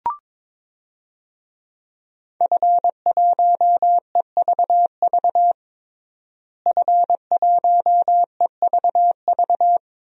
Morse code, heard as a keyed tone, about 22 wpm, 700 Hz. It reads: F1EVV F1EVV
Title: DXCC-100 Call Signs - 5 Characters - Repeated 2x - 22wpm